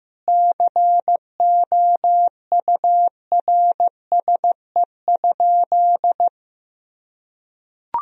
Morse code, heard as a keyed tone, about 15 wpm, 700 Hz. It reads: COURSE?
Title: Sentences from Top 1000 Words - Review Part 2 - 15wpm